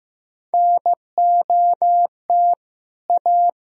Morse code, heard as a keyed tone, about 16 wpm, 700 Hz. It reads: NOT A
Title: Sentences from Top 1000 Words - Review Part 4 - 15wpm